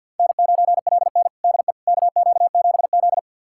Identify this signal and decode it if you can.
Morse code, heard as a keyed tone, about 47 wpm, 700 Hz. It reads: N9LN DE KQ7Z
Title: USA Callsign DE Callsign - 4 Characters - 50wpm